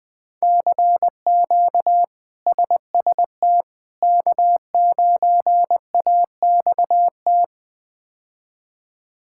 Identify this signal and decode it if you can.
Morse code, heard as a keyed tone, about 20 wpm, 700 Hz. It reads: CQ SST K9AXT
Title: K1USN SST QSO Practice - 20wpm